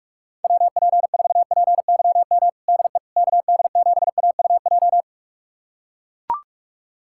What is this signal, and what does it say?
Morse code, heard as a keyed tone, about 41 wpm, 700 Hz. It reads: WP4PYM DE KD7AUJ
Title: USA Callsign DE Callsign - 6 Characters - 45wpm